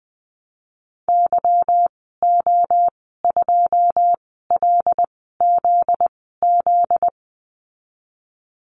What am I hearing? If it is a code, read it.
Morse code, heard as a keyed tone, about 20 wpm, spaced as for 15 wpm, 700 Hz. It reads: YO2LZZ
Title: 245 Lesson 39g - Letter z - Call Signs 20wpm